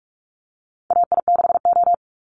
Morse code, heard as a keyed tone, about 44 wpm, 700 Hz. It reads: AI6O
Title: Call Signs Repeated 2x - 45wpm